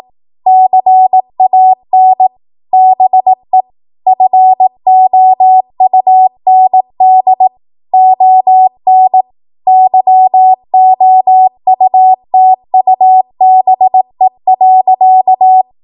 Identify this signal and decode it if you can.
Morse code, heard as a keyed tone, about 18 wpm, 750 Hz. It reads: CAN BE FOUND ON YOUTUBE.
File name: Morse / 18WPM